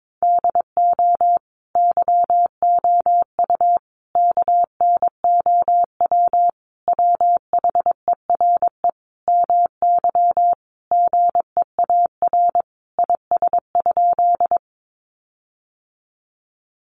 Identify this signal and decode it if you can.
Morse code, heard as a keyed tone, about 22 wpm, 700 Hz. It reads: DO YOU KNOW WHERE MY GEAR IS?
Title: Sentences from Top 1400 Words 22wpm